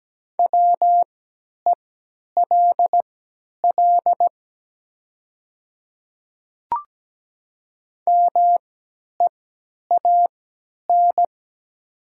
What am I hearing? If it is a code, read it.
Morse code, heard as a keyed tone, about 17 wpm, spaced as for 10 wpm, 700 Hz. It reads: WELL MEAN
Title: 3x Character Spacing - Top 500 Words - 17wpm